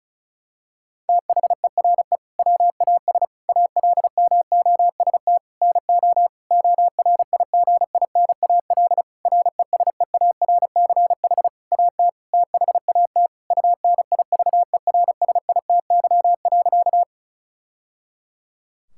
Morse code, heard as a keyed tone, about 34 wpm, 700 Hz. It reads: THERE WAS ALMOST NO ORIGINAL RESEARCH AT THAT UNIVERSITY.
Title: Sentences from Top 1000 Words - Review Part 6 - 35wpm